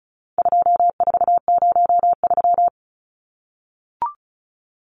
Morse code, heard as a keyed tone, about 34 wpm, 700 Hz. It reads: 2403
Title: Numbers - 4-Digit 35wpm